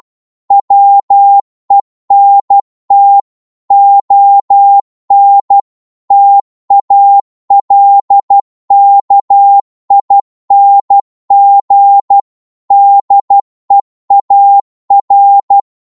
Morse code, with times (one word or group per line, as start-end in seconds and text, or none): 0.5-3.2 WENT
3.7-5.6 ON
6.1-12.2 TALKING
12.7-15.6 DEAR